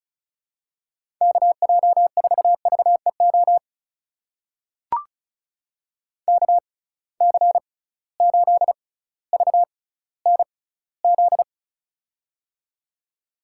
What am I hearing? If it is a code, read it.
Morse code, heard as a keyed tone, about 36 wpm, spaced as for 18 wpm, 700 Hz. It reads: KJ4VEO KC8VNZ
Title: USA Call Signs - 6 Characters - 6x Character Spacing - 35wpm